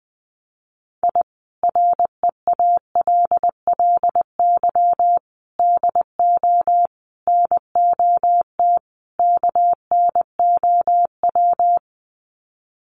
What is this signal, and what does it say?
Morse code, heard as a keyed tone, about 19 wpm, 700 Hz. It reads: I REALLY DO NOT KNOW
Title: Sets of 5 Words Encore 20wpm